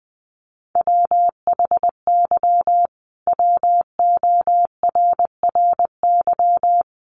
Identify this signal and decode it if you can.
Morse code, heard as a keyed tone, about 20 wpm, 700 Hz. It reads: WHY WORRY
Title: Sentences from Top 1100 Words 20wpm